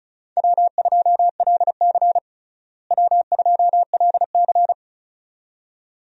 Morse code, heard as a keyed tone, about 36 wpm, 700 Hz. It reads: W2LC W2LC
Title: USA Call Signs - 4 Characters - Repeated 2x - 35wpm